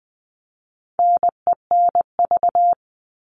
Morse code, heard as a keyed tone, about 20 wpm, 700 Hz. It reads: NENV
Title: All Cut Numbers - 4-Digit 20wpm